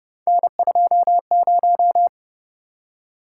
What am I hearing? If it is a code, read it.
Morse code, heard as a keyed tone, about 30 wpm, 700 Hz. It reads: N20